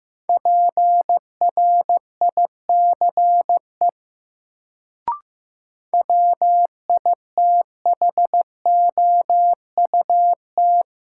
Morse code, heard as a keyed tone, about 15 wpm, 700 Hz. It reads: PRICE WITHOUT
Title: Sentences from Top 600 Words - New Word Drill - 15wpm